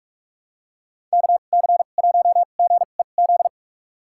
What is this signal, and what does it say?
Morse code, heard as a keyed tone, about 46 wpm, spaced as for 32 wpm, 700 Hz. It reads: KC1GEZ
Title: USA Call Signs - 2x Character Spacing - 45wpm